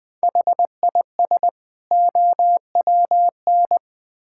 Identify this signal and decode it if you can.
Morse code, heard as a keyed tone, about 20 wpm, 700 Hz. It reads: HIS OWN
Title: Build A Phrase - 2-Word Phrases - 20wpm